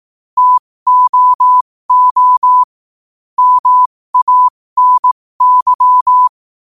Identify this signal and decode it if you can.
Morse code, heard as a keyed tone, about 18 wpm, spaced as for 15 wpm, 1000 Hz. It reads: TOO MANY
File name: qotdfiles / 15wpm